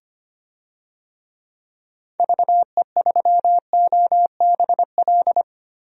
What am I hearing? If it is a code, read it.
Morse code, heard as a keyed tone, about 25 wpm, 700 Hz. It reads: VE3OBL
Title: DXCC-100 Call Signs - 6 Characters - Repeated 2x - 25wpm